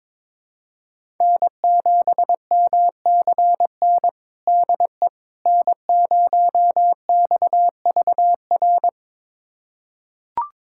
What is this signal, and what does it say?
Morse code, heard as a keyed tone, about 21 wpm, 700 Hz. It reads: N7MCN DE N0XVR
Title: USA Callsign DE Callsign - 5 Characters - 22wpm